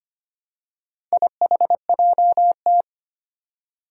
Morse code, heard as a keyed tone, about 25 wpm, 700 Hz. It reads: IHJT
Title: Four Challenging Letter-Number Combinations 25wpm